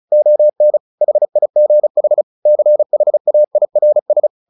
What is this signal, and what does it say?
Morse code, heard as a keyed tone, about 34 wpm, 600 Hz. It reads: ON HIGH CHAIRS
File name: the old man and the sea 2 35wpm